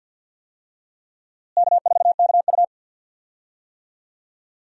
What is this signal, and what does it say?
Morse code, heard as a keyed tone, about 52 wpm, 700 Hz. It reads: K4KU
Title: USA Call Signs - 4 Characters - Repeated 3x - 50wpm